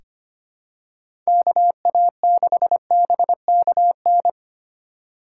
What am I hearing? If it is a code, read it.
Morse code, heard as a keyed tone, about 25 wpm, 700 Hz. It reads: KA6BKN
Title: USA Call Signs - 2x Character Spacing - 25wpm